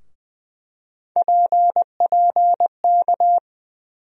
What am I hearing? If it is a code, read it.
Morse code, heard as a keyed tone, about 21 wpm, 700 Hz. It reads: PPK
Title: USA Call Sign Postfixes - 3 Letters 20wpm